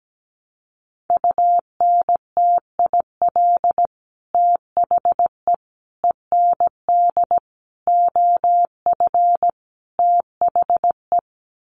Morse code, heard as a keyed tone, about 17 wpm, 700 Hz. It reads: UNTIL THE END OF THE
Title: Sets of 5 Words Encore 17wpm